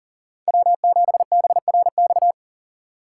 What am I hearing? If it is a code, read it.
Morse code, heard as a keyed tone, about 50 wpm, 700 Hz. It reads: W7BRX
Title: Speed Racing - USA 1x3 Call Signs 40wpm